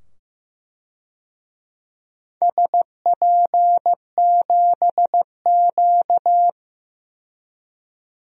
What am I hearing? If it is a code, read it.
Morse code, heard as a keyed tone, about 15 wpm, 700 Hz. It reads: SP7Q
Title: DXCC-100 Call Signs - 4 Characters - 15wpm